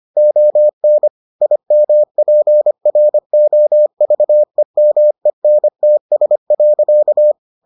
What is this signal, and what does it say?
Morse code, heard as a keyed tone, about 25 wpm, 600 Hz. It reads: ON IMPROVEMENTS.